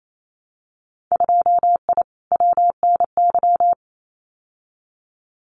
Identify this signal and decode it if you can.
Morse code, heard as a keyed tone, about 28 wpm, 700 Hz. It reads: 2I WNY
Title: Winter Field Day Exchanges 28wpm